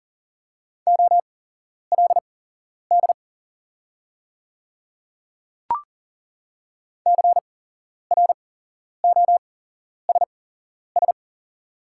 Morse code, heard as a keyed tone, about 41 wpm, spaced as for 11 wpm, 700 Hz. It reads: OLD CROSS
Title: 8x Character Spacing - Top 300 Words - 40wpm